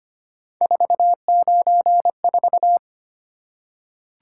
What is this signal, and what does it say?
Morse code, heard as a keyed tone, about 25 wpm, 700 Hz. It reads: 494
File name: Three Letter-Number Combinations - 25wpm